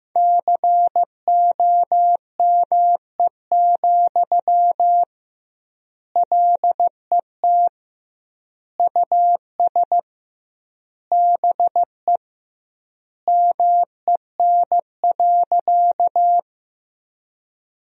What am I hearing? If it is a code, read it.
Morse code, heard as a keyed tone, about 15 wpm, 700 Hz. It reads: COME, LET US BE MEN.
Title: 2x Word-Spacing - Sentences from Top 300 Words - 15wpm